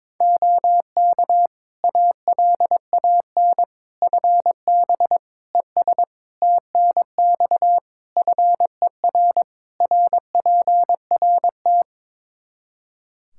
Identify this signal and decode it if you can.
Morse code, heard as a keyed tone, about 22 wpm, 700 Hz. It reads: OK ALAN FB ES TNX FER RPRT
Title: Practice QSO - Exchange 2 - Info - 22wpm